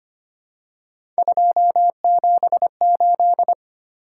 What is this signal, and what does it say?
Morse code, heard as a keyed tone, about 25 wpm, 700 Hz. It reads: 278
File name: Three Letter-Number Combinations - 25wpm